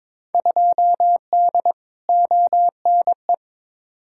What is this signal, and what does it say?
Morse code, heard as a keyed tone, about 22 wpm, 700 Hz. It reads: 2D ONE
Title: Field Day Exchanges - 22wpm